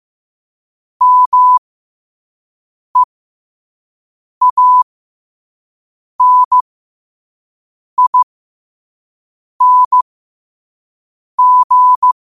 Morse code, heard as a keyed tone, about 15 wpm, spaced as for 5 wpm, 1000 Hz. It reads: MEANING
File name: 5wpm